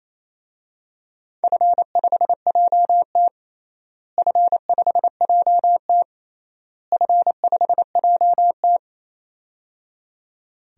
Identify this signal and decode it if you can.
Morse code, heard as a keyed tone, about 28 wpm, 700 Hz. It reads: F5JT F5JT F5JT
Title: DXCC-100 Call Signs - 4 Characters - Repeated 3x - 28wpm